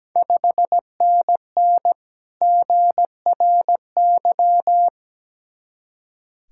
Morse code, heard as a keyed tone, about 17 wpm, 700 Hz. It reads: 5NN GRY
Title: Ontario QSO Party Exchange 17wpm